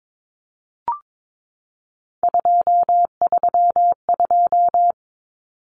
Morse code, heard as a keyed tone, about 22 wpm, 700 Hz. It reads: 232
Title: Numbers - 3-Digit 22wpm